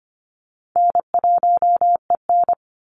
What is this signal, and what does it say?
Morse code, heard as a keyed tone, about 25 wpm, 700 Hz. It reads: N1EN
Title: Easy Call Signs Repeated 2x - 25wpm